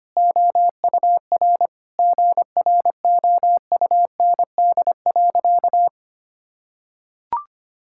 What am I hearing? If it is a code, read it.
Morse code, heard as a keyed tone, about 25 wpm, 700 Hz. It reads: OUR GROUND.
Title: Sentences from Top 1000 Words - Review Part 5 - 25wpm